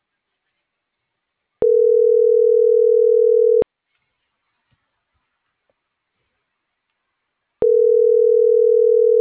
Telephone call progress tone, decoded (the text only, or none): ringback tone